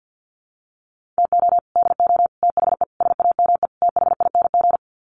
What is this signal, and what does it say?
Morse code, heard as a keyed tone, about 50 wpm, 700 Hz. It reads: TO DO THE SAME THING
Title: Sets of 5 Words Encore 50wpm